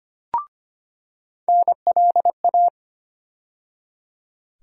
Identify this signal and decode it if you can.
Morse code, heard as a keyed tone, about 25 wpm, 700 Hz. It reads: NLA
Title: USA Call Sign Postfixes - 3 Letters 25wpm